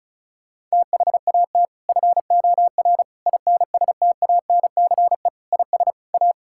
Morse code, heard as a keyed tone, about 36 wpm, 700 Hz. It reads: THAT FOR INSTANCE IS A
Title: Sentences from Top 1000 Words - Review Part 5 - 35wpm